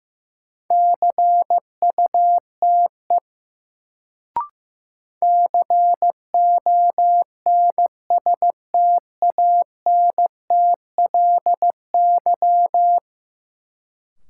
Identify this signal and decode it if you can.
Morse code, heard as a keyed tone, about 16 wpm, 700 Hz. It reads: CUTE CONSTANTLY
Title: Sentences from Top 1000 Words - Review Part 7 - New Word Drill - 15wpm